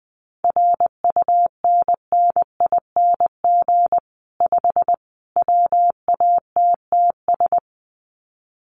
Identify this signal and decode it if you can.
Morse code, heard as a keyed tone, about 20 wpm, 700 Hz. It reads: RUNNING 5 WATTS
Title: Common QSO Phrases 20wpm